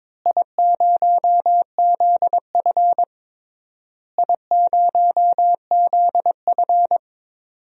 Morse code, heard as a keyed tone, about 22 wpm, 700 Hz. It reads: I0ZF I0ZF
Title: DXCC-100 Call Signs - 4 Characters - Repeated 3x - 22wpm